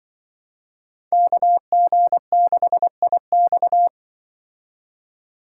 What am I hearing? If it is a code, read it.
Morse code, heard as a keyed tone, about 24 wpm, 700 Hz. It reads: KG6IX
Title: Speed Racing - USA 2x2 Call Signs 20wpm